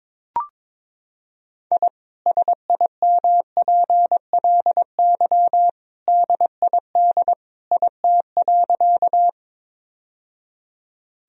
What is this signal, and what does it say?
Morse code, heard as a keyed tone, about 22 wpm, 700 Hz. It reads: I SIMPLY DID IT.
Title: Sentences from Top 1000 Words - Review Part 4 - 22wpm